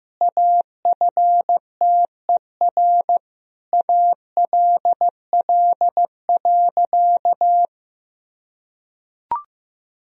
Morse code, heard as a keyed tone, about 15 wpm, 700 Hz. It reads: AFTER ALL.
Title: Sentences from Top 1000 Words - Review Part 2 - 15wpm